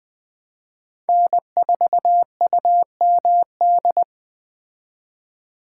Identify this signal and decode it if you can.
Morse code, heard as a keyed tone, about 21 wpm, 700 Hz. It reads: N4UMD
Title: Speed Racing - USA Call Signs 17wpm